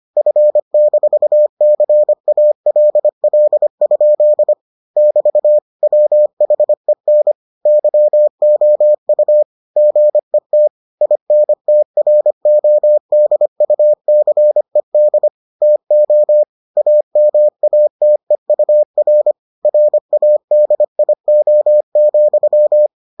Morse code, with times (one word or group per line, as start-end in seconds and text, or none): 0.2-4.5 F-CALL?
5.0-5.6 <BT>
5.8-7.3 WHEN
7.7-9.4 YOU
9.8-10.7 GET
11.0-15.3 INTRODUCED
15.6-16.4 TO
16.8-19.3 AMATEUR
19.7-22.9 RADIO,